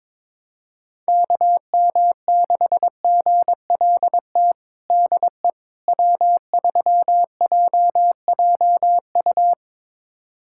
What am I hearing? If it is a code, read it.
Morse code, heard as a keyed tone, about 22 wpm, 700 Hz. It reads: KM6GLT DE W3JJU
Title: USA Callsign DE Callsign 22wpm